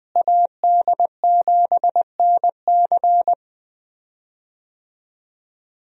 Morse code, heard as a keyed tone, about 20 wpm, 700 Hz. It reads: AD7NC
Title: USA Call Signs - 5 Characters - Repeated 3x - 20wpm